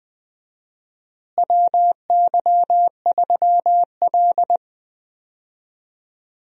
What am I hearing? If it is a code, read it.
Morse code, heard as a keyed tone, about 20 wpm, 700 Hz. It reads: WY3L
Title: USA Call Signs - 4 Characters - Repeated 3x - 20wpm